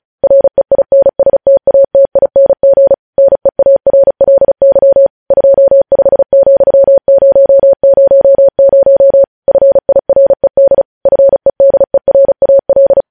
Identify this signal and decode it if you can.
Morse code, heard as a keyed tone, about 36 wpm, 550 Hz. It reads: REINSTATING NEARLY 25,000 FIRED FEDERAL
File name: podcast / News headlines-1x-35x35